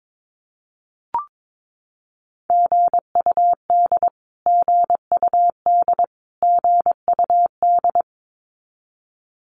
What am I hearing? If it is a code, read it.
Morse code, heard as a keyed tone, about 22 wpm, 700 Hz. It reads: GUD GUD GUD
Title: Abbreviations Repeated 3x - 22wpm